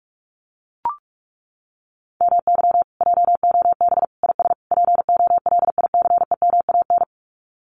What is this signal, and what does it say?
Morse code, heard as a keyed tone, about 44 wpm, 700 Hz. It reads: MY JOB IS POLICEMAN
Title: Common QSO Phrases Encore 45wpm